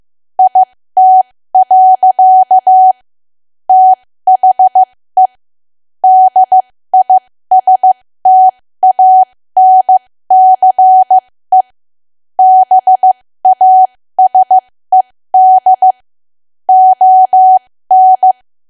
Morse code, heard as a keyed tone, about 15 wpm, spaced as for 13 wpm, 750 Hz. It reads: IT. THE DISTANCE BASED ON
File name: cwa / W1AW-Level2 / 13WPM